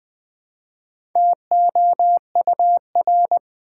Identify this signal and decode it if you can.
Morse code, heard as a keyed tone, about 20 wpm, 700 Hz. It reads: TOUR
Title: Build A Word - 2 Letters At A Time - 4-Letter Words - 20wpm